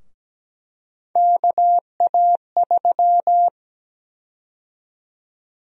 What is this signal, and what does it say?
Morse code, heard as a keyed tone, about 17 wpm, 700 Hz. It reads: KA3
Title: USA Call Sign Prefixes - Repeated 2x - 17wpm